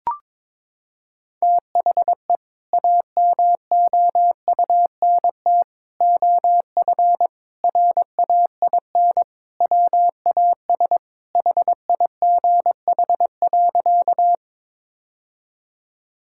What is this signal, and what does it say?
Morse code, heard as a keyed tone, about 22 wpm, 700 Hz. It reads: THE AMOUNT OF RAIN WAS HIGH.
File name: practice / 22wpm / Sentences with 3-Word Phrases - 22wpm